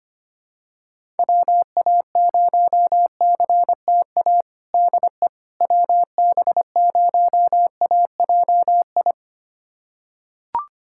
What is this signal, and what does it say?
Morse code, heard as a keyed tone, about 25 wpm, 700 Hz. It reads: WA0CTA DE WB0AJI
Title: USA Callsign DE Callsign - 6 Characters - 25wpm